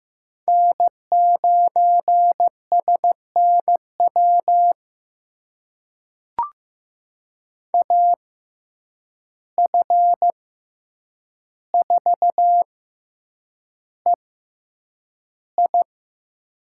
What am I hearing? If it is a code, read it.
Morse code, heard as a keyed tone, about 15 wpm, spaced as for 8 wpm, 700 Hz. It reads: N9SNW AF4EI
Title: USA Call Signs - 5 Characters - 6x Character Spacing - 15wpm